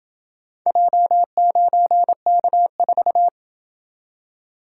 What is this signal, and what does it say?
Morse code, heard as a keyed tone, about 27 wpm, 700 Hz. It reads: J9K4